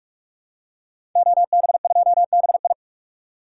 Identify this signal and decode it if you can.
Morse code, heard as a keyed tone, about 42 wpm, 700 Hz. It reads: OB2BI
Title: Code Groups - 5 Letter:Numbers - 45wpm